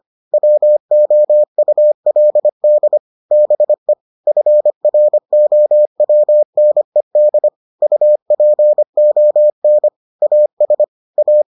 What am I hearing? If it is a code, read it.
Morse code, heard as a keyed tone, about 25 wpm, 600 Hz. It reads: WOULD BE FROWNED UPON AS A